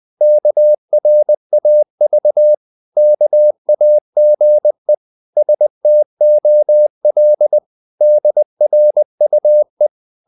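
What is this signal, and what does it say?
Morse code, heard as a keyed tone, about 20 wpm, 600 Hz. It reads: KRAV KAGE STOL DRUE
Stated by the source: Morse code practice of short Danish words at 20 wpm, from CW med Gnister.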